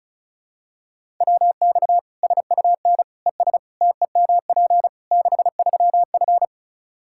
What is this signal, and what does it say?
Morse code, heard as a keyed tone, about 36 wpm, 700 Hz. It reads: WX SUN ES TEMP 63F
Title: Practice QSO - Exchange 2 - Info - 35wpm